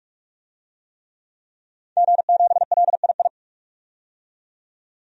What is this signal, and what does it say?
Morse code, heard as a keyed tone, about 42 wpm, 700 Hz. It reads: G7LII
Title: DXCC-100 Call Signs - 5 Characters - 45wpm